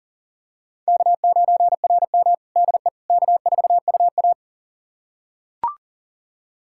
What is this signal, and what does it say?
Morse code, heard as a keyed tone, about 40 wpm, 690 Hz. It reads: K9RM DE K4UA
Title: USA Callsign DE Callsign - 4 Characters - 40wpm